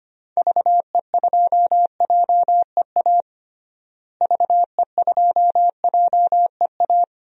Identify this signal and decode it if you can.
Morse code, heard as a keyed tone, about 25 wpm, 700 Hz. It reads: VE2JEA VE2JEA
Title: DXCC-100 Call Signs - 6 Characters - Repeated 2x - 25wpm